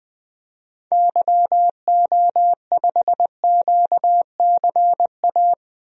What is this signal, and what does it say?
Morse code, heard as a keyed tone, about 20 wpm, 700 Hz. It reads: YO5QCA
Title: DXCC-100 Call Signs - 6 Characters - 20wpm